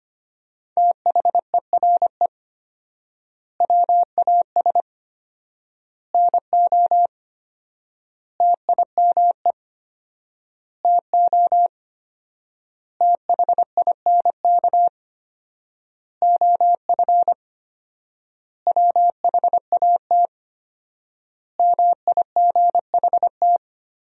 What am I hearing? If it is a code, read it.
Morse code, heard as a keyed tone, about 25 wpm, 700 Hz. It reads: THERE WAS NO TIME TO THINK OF WHAT MIGHT
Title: 4x Word-Spacing - Sentences from Top 400 Words - 25wpm